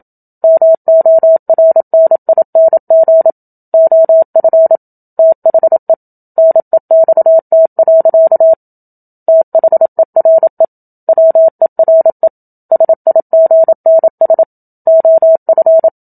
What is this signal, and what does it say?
Morse code, heard as a keyed tone, about 27 wpm, 650 Hz. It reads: MORNING OF THE NEXT. THERE WERE SIGNS OF
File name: Book Two / 01-Chapter 1 - 27 WPM